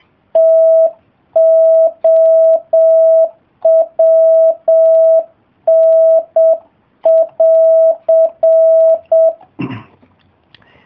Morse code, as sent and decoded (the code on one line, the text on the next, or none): - --- .-- -. .-.-.
TOWN<AR>